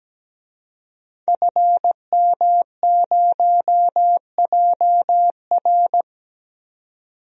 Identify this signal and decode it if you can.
Morse code, heard as a keyed tone, about 17 wpm, 700 Hz. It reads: FM0JR
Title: DXCC-100 Call Signs - 5 Characters - 17wpm